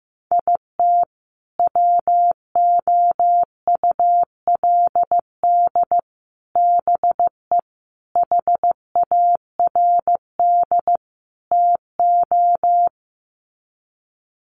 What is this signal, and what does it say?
Morse code, heard as a keyed tone, about 14 wpm, 700 Hz. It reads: IT WOULD BE HARD TO
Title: Sets of 5 Words Encore2 15wpm